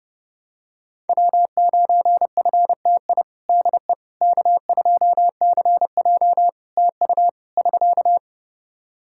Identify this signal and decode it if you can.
Morse code, heard as a keyed tone, about 31 wpm, 700 Hz. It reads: W9FTI DE K2CJ TU <SK>
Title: K1USN SST QSO Practice 30wpm